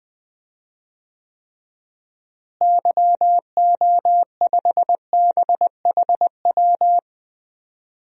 Morse code, heard as a keyed tone, about 20 wpm, 700 Hz. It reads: YO5BHW
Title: DXCC-100 Call Signs - 6 Characters - Repeated 2x - 20wpm